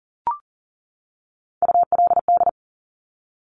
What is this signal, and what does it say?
Morse code, heard as a keyed tone, about 42 wpm, 690 Hz. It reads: ULD
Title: Top Three Letter Combinations in Words 40wpm